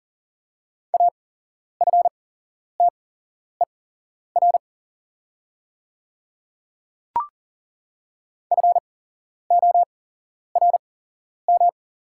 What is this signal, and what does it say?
Morse code, heard as a keyed tone, about 40 wpm, spaced as for 11 wpm, 700 Hz. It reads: AFTER FORM
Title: 8x Character Spacing - Top 300 Words - 40wpm